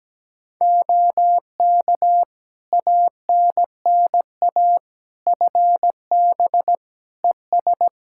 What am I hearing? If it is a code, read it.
Morse code, heard as a keyed tone, about 17 wpm, 700 Hz. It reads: OK ANNA FB ES
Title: Practice QSO - Exchange 2 - Info - 17wpm